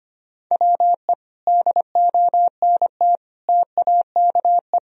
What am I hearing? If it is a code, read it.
Morse code, heard as a keyed tone, about 25 wpm, 700 Hz. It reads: WE DONT TAKE
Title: Sentences from Top 1000 Words - Review Part 4 - 25wpm